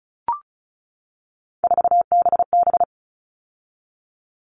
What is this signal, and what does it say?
Morse code, heard as a keyed tone, about 34 wpm, 700 Hz. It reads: 4BB
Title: All Cut Numbers - 3-Digit 35wpm